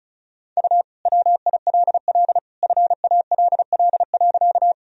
Morse code, heard as a keyed tone, about 34 wpm, 700 Hz. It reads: U WILL FALL.
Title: Sentences from Top 500 Words and Abbreviations 35wpm